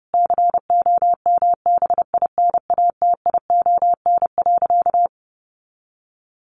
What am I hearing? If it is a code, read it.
Morse code, heard as a keyed tone, about 30 wpm, 700 Hz. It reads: COMBINATION.
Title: Sentences from Top 1300 Words 30wpm